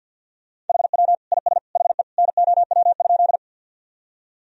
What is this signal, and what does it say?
Morse code, heard as a keyed tone, about 48 wpm, 700 Hz. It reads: HW IS HE NOW?
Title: Sentences from Top 100 Words and Abbreviations 50wpm